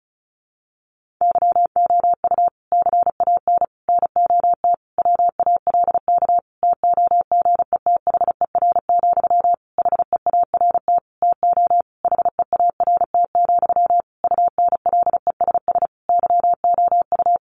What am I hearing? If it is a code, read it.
Morse code, heard as a keyed tone, about 34 wpm, 700 Hz. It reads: YOU CAN NOT WALK TOGETHER, HEART TO HEART, UNLESS YOU
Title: Sentences from Top 1500 Words 35wpm